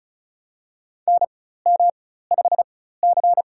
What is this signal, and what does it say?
Morse code, heard as a keyed tone, about 36 wpm, spaced as for 16 wpm, 700 Hz. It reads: NM5C
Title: USA Call Signs - 4x Character Spacing - 35wpm